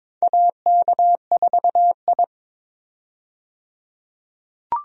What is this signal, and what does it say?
Morse code, heard as a keyed tone, about 22 wpm, 700 Hz. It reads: AK4I
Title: USA Call Signs - 4 Characters - 22wpm